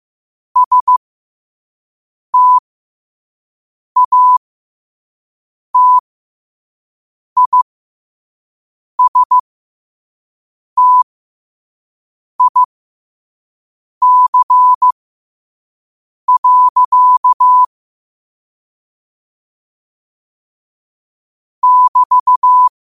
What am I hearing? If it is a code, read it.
Morse code, heard as a keyed tone, about 15 wpm, spaced as for 5 wpm, 1000 Hz. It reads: STATISTIC. <BT>
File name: 5wpm